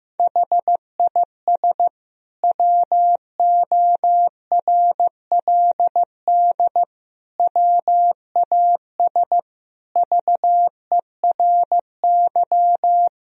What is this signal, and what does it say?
Morse code, heard as a keyed tone, about 15 wpm, 700 Hz. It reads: HIS WORLD WAS VERY
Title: Sentences from Top 1000 Words - Review Part 1 - 15wpm